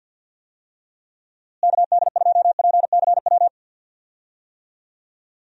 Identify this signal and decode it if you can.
Morse code, heard as a keyed tone, about 46 wpm, 700 Hz. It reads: KD2PCW
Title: USA Call Signs - 6 Characters - Repeated 2x - 50wpm